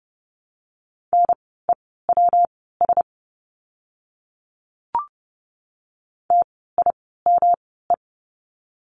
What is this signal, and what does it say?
Morse code, heard as a keyed tone, about 30 wpm, spaced as for 17 wpm, 700 Hz. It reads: NEWS TIME
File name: course / 30:10wpm / 115 Lesson 19f - Letter w - Words 30wpm